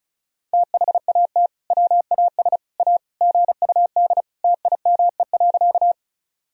Morse code, heard as a keyed tone, about 34 wpm, 700 Hz. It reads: THAT WAS A GUD TIME.
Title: Sentences from Top 200 Words and Abbreviations 35wpm